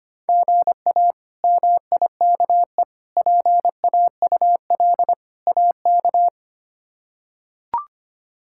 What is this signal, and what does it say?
Morse code, heard as a keyed tone, about 25 wpm, 700 Hz. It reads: GA MIKE PAUL AK
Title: K1USN SST QSO Practice - 25wpm